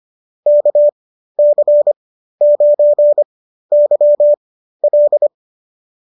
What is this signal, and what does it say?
Morse code, heard as a keyed tone, about 25 wpm, spaced as for 13 wpm, 590 Hz. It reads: KC9YL